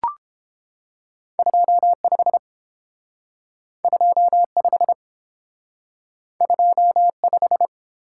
Morse code, heard as a keyed tone, about 29 wpm, 700 Hz. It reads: 25 25 25